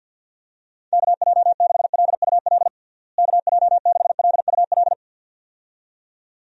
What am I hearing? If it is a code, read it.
Morse code, heard as a keyed tone, about 51 wpm, 700 Hz. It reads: KJ6LUL KJ6LUL
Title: USA Call Signs - 6 Characters - Repeated 2x - 50wpm